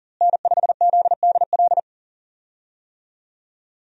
Morse code, heard as a keyed tone, about 40 wpm, 700 Hz. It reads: N5ZDL
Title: USA Call Signs - 5 Characters - Repeated 3x - 40wpm